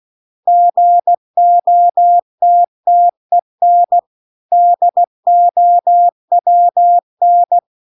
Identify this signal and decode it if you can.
Morse code, heard as a keyed tone, about 16 wpm, 700 Hz. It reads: GOTTEN DOWN